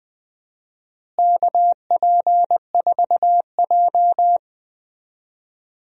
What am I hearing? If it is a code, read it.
Morse code, heard as a keyed tone, about 20 wpm, 700 Hz. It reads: KP4J
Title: DXCC-100 Call Signs - 4 Characters - Repeated 2x - 20wpm